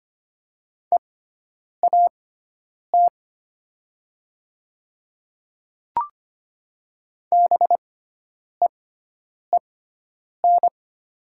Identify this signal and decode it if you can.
Morse code, heard as a keyed tone, about 25 wpm, spaced as for 9 wpm, 700 Hz. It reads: EAT BEEN
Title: 6x Character Spacing - Top 500 Words - 25wpm